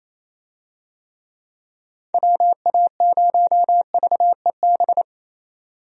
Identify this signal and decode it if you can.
Morse code, heard as a keyed tone, about 28 wpm, 700 Hz. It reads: WA0VEB
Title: USA Call Signs - 6 Characters - 28wpm